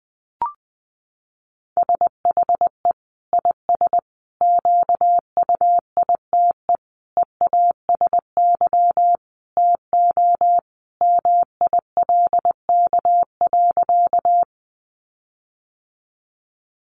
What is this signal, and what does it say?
Morse code, heard as a keyed tone, about 20 wpm, 700 Hz. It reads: SHE IS QUITE EASY TO MILK.
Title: Sentences from Top 1500 Words 20wpm